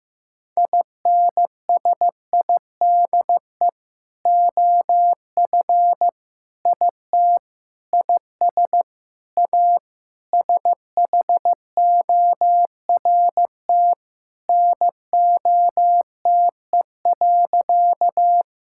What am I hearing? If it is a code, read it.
Morse code, heard as a keyed tone, about 15 wpm, 700 Hz. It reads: INSIDE OF IT IS A SHORT NOTE.
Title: Sentences from Top 1000 Words - Review Part 6 - 15wpm